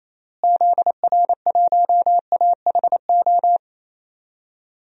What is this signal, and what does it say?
Morse code, heard as a keyed tone, about 27 wpm, 700 Hz. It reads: ZR1AHO
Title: DXCC-100 Call Signs - 6 Characters - Repeated 2x - 28wpm